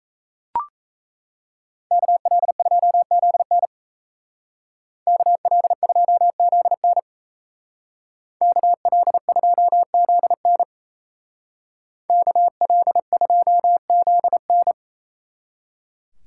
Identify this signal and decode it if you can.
Morse code, heard as a keyed tone, about 33 wpm, 700 Hz. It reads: KL2ZN KL2ZN KL2ZN KL2ZN
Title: Speed Racing - USA 2x2 Call Signs 28wpm